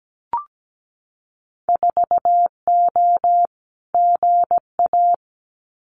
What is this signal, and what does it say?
Morse code, heard as a keyed tone, about 17 wpm, 700 Hz. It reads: 4O GA
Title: Winter Field Day Exchanges 17wpm